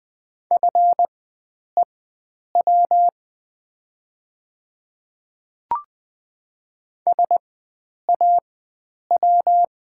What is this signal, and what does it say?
Morse code, heard as a keyed tone, about 20 wpm, spaced as for 9 wpm, 700 Hz. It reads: FEW SAW
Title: 4x Character Spacing - Top 300 Words - 20wpm